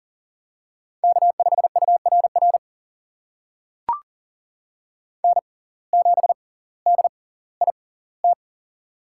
Morse code, heard as a keyed tone, about 38 wpm, spaced as for 20 wpm, 700 Hz. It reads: K5URR N7DIT
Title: USA Call Signs - 5 Characters - 6x Character Spacing - 40wpm